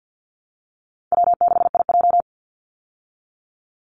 Morse code, heard as a keyed tone, about 48 wpm, 700 Hz. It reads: W6IJ
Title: Easy Call Signs Repeated 2x - 50wpm